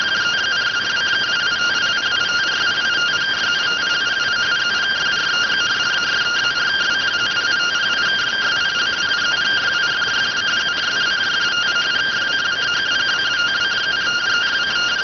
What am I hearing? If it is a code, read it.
Two-tone teleprinter signal, about 170 Hz shift.